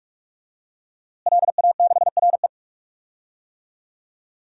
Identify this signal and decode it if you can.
Morse code, heard as a keyed tone, about 45 wpm, 700 Hz. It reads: LA6RE
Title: DXCC-100 Call Signs - 5 Characters - Repeated 3x - 45wpm